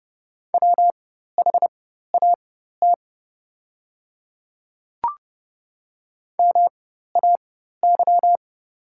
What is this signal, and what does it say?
Morse code, heard as a keyed tone, about 30 wpm, spaced as for 14 wpm, 700 Hz. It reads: WHAT MAY
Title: 4x Character Spacing - Top 200 Words - 30wpm